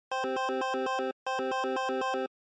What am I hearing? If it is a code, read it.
There is no code.